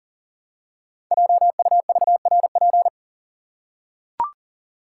Morse code, heard as a keyed tone, about 42 wpm, 700 Hz. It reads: JUVRP
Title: Code Groups - 5 Letters - 40wpm